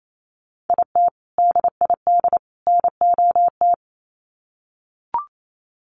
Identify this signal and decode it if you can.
Morse code, heard as a keyed tone, about 28 wpm, 700 Hz. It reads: IT DID NOT
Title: Sets of 3 Words 28wpm